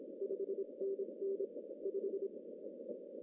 Morse code, heard as a keyed tone, about 25 wpm, 390 Hz. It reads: 5NN 5
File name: CW / cqww-cw-k5tr